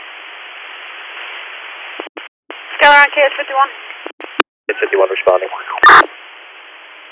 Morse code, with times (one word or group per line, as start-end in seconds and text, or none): none